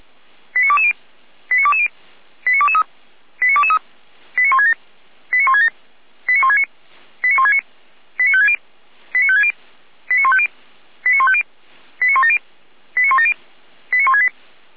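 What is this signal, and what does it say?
Five-tone selective call (ZVEI), heard as quick runs of five tones, tones of about 70 ms. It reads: 89220 89220 89223 89223 89167 89167 89169 89169 89560 89560 89150 89150 89170 89170 89168
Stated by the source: ZVEI five-tone selective call received on an SDR.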